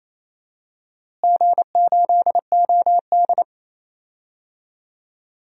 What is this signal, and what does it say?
Morse code, heard as a keyed tone, about 28 wpm, 700 Hz. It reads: G8OD
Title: DXCC-100 Call Signs - 4 Characters - Repeated 3x - 28wpm